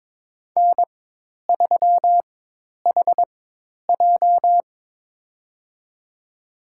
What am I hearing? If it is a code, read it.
Morse code, heard as a keyed tone, about 22 wpm, spaced as for 10 wpm, 700 Hz. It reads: N3HJ
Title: USA Call Signs - 4 Characters - 4x Character Spacing - 22wpm